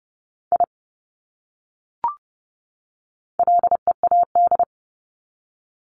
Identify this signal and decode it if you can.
Morse code, heard as a keyed tone, about 30 wpm, 700 Hz. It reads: I LEAD
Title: Top 500 Words 30wpm